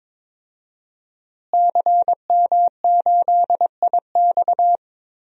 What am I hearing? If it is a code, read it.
Morse code, heard as a keyed tone, about 22 wpm, 700 Hz. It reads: CM8IX
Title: DXCC-100 Call Signs - 5 Characters - Repeated 2x - 22wpm